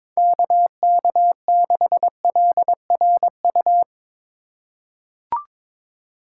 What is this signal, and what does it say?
Morse code, heard as a keyed tone, about 22 wpm, 700 Hz. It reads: KK6LRU
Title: USA Call Signs - 6 Characters - 3x Character Spacing - 22wpm